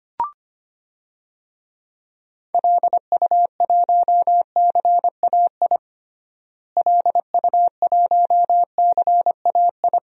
Morse code, heard as a keyed tone, about 25 wpm, 700 Hz. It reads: LU1CAI LU1CAI
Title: DXCC-100 Call Signs - 6 Characters - Repeated 3x - 25wpm